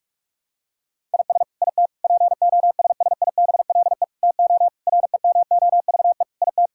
Morse code, heard as a keyed tone, about 46 wpm, 700 Hz. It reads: IS IT POSSIBLE TO REMOVE IT
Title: Sentences from Top 1000 Words - Review Part 5 - 45wpm